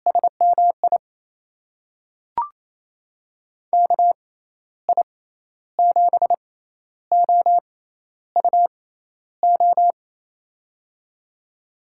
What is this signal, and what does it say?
Morse code, heard as a keyed tone, about 28 wpm, spaced as for 12 wpm, 700 Hz. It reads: SMI KI7OUO
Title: USA Call Signs - 6 Characters - 6x Character Spacing - 28wpm